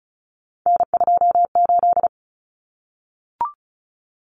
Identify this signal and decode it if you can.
Morse code, heard as a keyed tone, about 34 wpm, 700 Hz. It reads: N28
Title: Common Cut Numbers - 3-Digit 35wpm